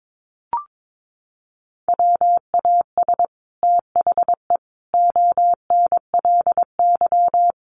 Morse code, heard as a keyed tone, about 22 wpm, 700 Hz. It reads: WAS THE ONLY
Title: Sets of 5 Words Encore2 22wpm